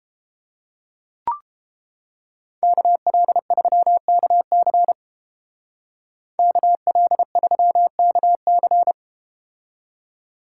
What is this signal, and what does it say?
Morse code, heard as a keyed tone, about 31 wpm, 700 Hz. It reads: KL3KC KL3KC